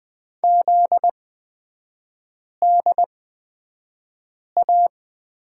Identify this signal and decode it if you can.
Morse code, heard as a keyed tone, about 20 wpm, spaced as for 5 wpm, 700 Hz. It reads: ZDA
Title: 250 Lesson 40e - BK - Three Characters 20wpm